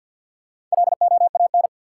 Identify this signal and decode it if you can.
Morse code, heard as a keyed tone, about 50 wpm, 700 Hz. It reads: LOAN
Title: Four Letter Words - 50wpm